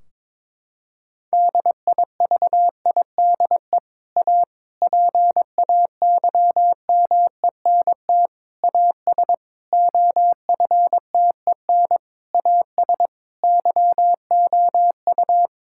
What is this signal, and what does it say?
Morse code, heard as a keyed tone, about 22 wpm, 700 Hz. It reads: DIVIDE A PAYMENT AS OFTEN AS YOU
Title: Sentences from Top 1600 Words 22wpm